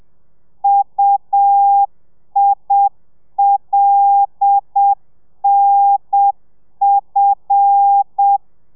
Morse code, heard as a keyed tone, about 7 wpm, 800 Hz. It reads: UILNF
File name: telegrafia / i2tfj / Lez5 30